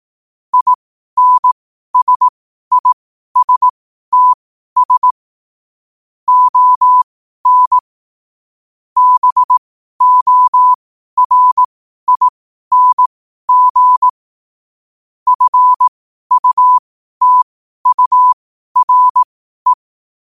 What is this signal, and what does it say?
Morse code, heard as a keyed tone, about 18 wpm, spaced as for 12 wpm, 1000 Hz. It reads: INSISTS ON BORING FUTURE
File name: qotdfiles / 12wpm